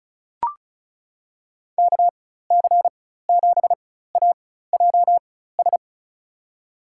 Morse code, heard as a keyed tone, about 34 wpm, spaced as for 16 wpm, 700 Hz. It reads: KC7AJS
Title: USA Call Signs - 6 Characters - 4x Character Spacing - 35wpm